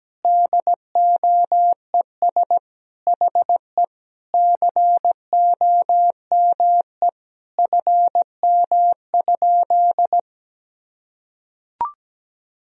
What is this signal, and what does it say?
Morse code, heard as a keyed tone, about 17 wpm, 700 Hz. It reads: DOES HE COME FM?
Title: Sentences from Top 300 Words and Abbreviations 17wpm